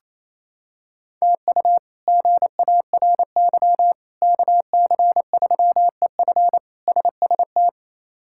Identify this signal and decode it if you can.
Morse code, heard as a keyed tone, about 28 wpm, 700 Hz. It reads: TU GARY KC3EF SST
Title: K1USN SST QSO Practice - 28wpm